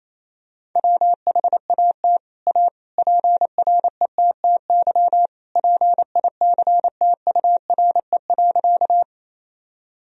Morse code, heard as a keyed tone, about 27 wpm, 700 Hz. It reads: WHAT A PRETTY PICTURE.
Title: Sentences from Top 1000 Words - Review Part 1 - 28wpm